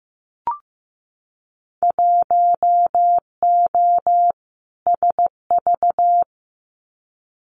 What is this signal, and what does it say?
Morse code, heard as a keyed tone, about 15 wpm, 700 Hz. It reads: 1O SV
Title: Winter Field Day Exchanges 15wpm